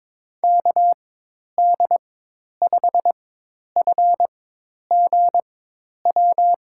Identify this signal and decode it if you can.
Morse code, heard as a keyed tone, about 21 wpm, spaced as for 10 wpm, 700 Hz. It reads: KD5FGW
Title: USA Call Signs - 4x Character Spacing - 22wpm